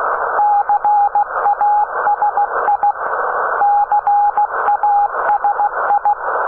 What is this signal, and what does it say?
Morse code, heard as a keyed tone, about 16 wpm, 820 Hz, with radio noise. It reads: CASI CASI
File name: CAS-2T 21h00 utc